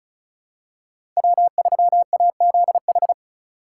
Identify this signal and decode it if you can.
Morse code, heard as a keyed tone, about 34 wpm, 690 Hz. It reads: W3AZH
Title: USA Call Signs - 5 Characters - 2x Character Spacing - 35wpm